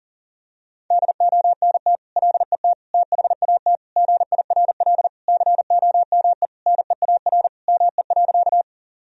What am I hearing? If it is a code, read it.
Morse code, heard as a keyed tone, about 40 wpm, 700 Hz. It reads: DONT LET THAT GIRL COME NEAR ME.
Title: Sentences from Top 1000 Words - Review Part 1 - 40wpm